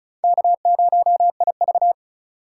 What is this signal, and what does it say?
Morse code, heard as a keyed tone, about 44 wpm, 700 Hz. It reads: K0IV